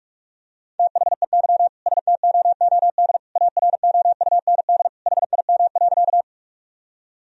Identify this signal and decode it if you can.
Morse code, heard as a keyed tone, about 41 wpm, 700 Hz. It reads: THEY STOOD AROUND HIM.
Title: Sentences from Top 1000 Words - Review Part 3 - 45wpm